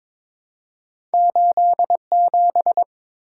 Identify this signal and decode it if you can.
Morse code, heard as a keyed tone, about 22 wpm, 700 Hz. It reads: 87
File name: practice / 22wpm / Two Letter-Number Combinations - 22wpm